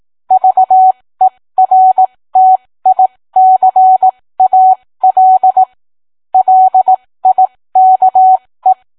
Morse code, heard as a keyed tone, about 18 wpm, spaced as for 15 wpm, 750 Hz. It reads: VERTICAL LIKE